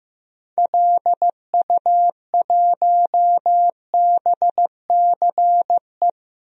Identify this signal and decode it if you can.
Morse code, heard as a keyed tone, about 15 wpm, 700 Hz. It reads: LU1BCE
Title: DXCC-100 Call Signs - 6 Characters - 15wpm